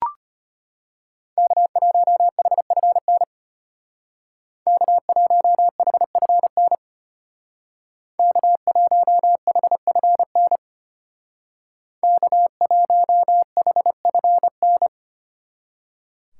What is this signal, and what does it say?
Morse code, heard as a keyed tone, about 32 wpm, 700 Hz. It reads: K1HFN K1HFN K1HFN K1HFN